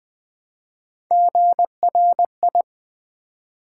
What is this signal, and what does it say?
Morse code, heard as a keyed tone, about 20 wpm, 700 Hz. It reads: GRI
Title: Build A Word - 1 Letter At A Time - 3 to 5-Letter Words - 20wpm